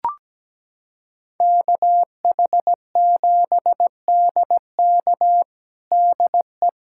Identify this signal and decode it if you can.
Morse code, heard as a keyed tone, about 17 wpm, 700 Hz. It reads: KH7DK DE
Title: USA Callsign DE Callsign - 5 Characters - 17wpm